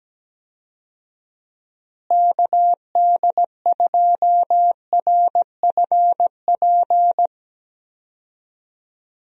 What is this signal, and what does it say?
Morse code, heard as a keyed tone, about 17 wpm, 700 Hz. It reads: KD2RFP
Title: USA Call Signs - 6 Characters - Repeated 3x - 17wpm